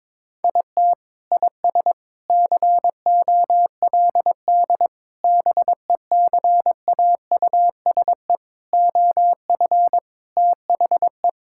Morse code, heard as a keyed tone, about 22 wpm, 700 Hz. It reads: IT IS COLD BECAUSE OF THE
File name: practice / 22wpm / Sentences with 3-Word Phrases - 22wpm